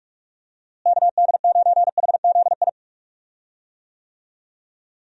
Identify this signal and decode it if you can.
Morse code, heard as a keyed tone, about 46 wpm, 700 Hz. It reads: KD9HZI
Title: USA Call Signs - 6 Characters - Repeated 2x - 45wpm